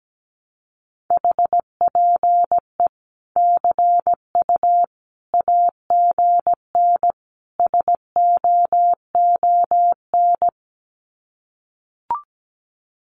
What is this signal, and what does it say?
Morse code, heard as a keyed tone, about 17 wpm, 700 Hz. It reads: HPE CU AGN SOON
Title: Common QSO Phrases 17wpm